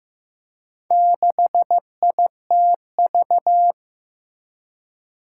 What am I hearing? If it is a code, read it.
Morse code, heard as a keyed tone, about 15 wpm, 700 Hz. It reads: 6ITV
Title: Four Challenging Letter-Number Combinations 15wpm